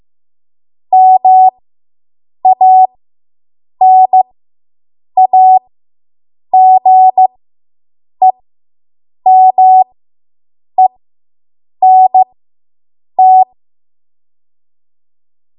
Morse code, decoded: MANAGEMENT

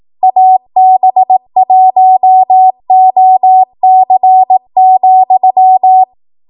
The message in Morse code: AB1OC,

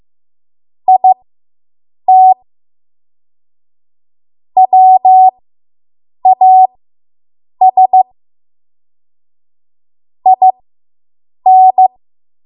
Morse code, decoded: IT WAS IN

